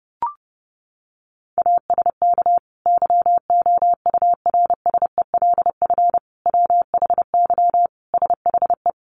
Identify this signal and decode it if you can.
Morse code, heard as a keyed tone, about 30 wpm, 700 Hz. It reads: ASK YOURSELF WHY SHE
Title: Sentences from Top 600 Words 30wpm